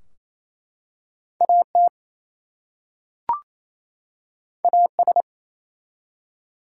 Morse code, heard as a keyed tone, about 27 wpm, 700 Hz. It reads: AT AS
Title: Two Letter Words - 28wpm